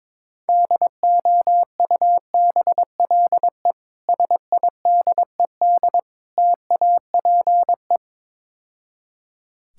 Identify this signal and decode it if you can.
Morse code, heard as a keyed tone, about 22 wpm, 700 Hz. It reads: DOUBLE SIDED TAPE
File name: Easy Words - Speed-Racing with Hints - 22wpm